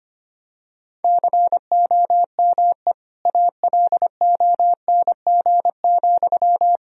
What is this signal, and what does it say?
Morse code, heard as a keyed tone, about 25 wpm, 700 Hz. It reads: COME ALONG,
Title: Sentences from Top 1000 Words - Review Part 1 - 25wpm